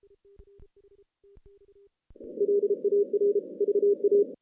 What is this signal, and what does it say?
Morse code, heard as a keyed tone, about 32 wpm, 410 Hz. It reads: LAR VA
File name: 2016 JAN CW NAQP / 2016 Jan naqp k5ot - 0-8